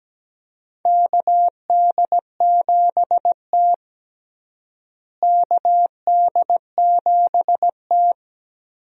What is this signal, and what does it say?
Morse code, heard as a keyed tone, about 17 wpm, 700 Hz. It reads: KD7T KD7T
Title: USA Call Signs - 4 Characters - Repeated 2x - 17wpm